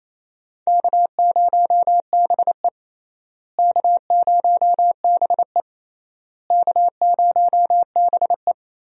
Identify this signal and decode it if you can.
Morse code, heard as a keyed tone, about 28 wpm, 700 Hz. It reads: K0BE K0BE K0BE
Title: USA Call Signs - 4 Characters - Repeated 3x - 28wpm